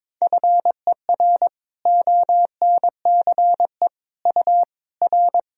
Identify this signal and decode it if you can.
Morse code, heard as a keyed tone, about 22 wpm, 700 Hz. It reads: FER ONCE U R
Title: Sentences from Top 300 Words and Abbreviations 22wpm